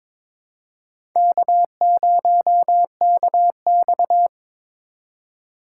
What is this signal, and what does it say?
Morse code, heard as a keyed tone, about 22 wpm, 700 Hz. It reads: K0KX
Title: K1USN SST QSO Practice - 22wpm